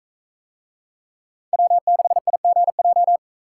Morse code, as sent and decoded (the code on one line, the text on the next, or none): .-- -.... .. --. .---
W6IGJ